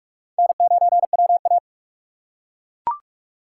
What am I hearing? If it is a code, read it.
Morse code, heard as a keyed tone, about 43 wpm, 700 Hz. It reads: N9WA
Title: USA Call Signs - 4 Characters - 2x Character Spacing - 45wpm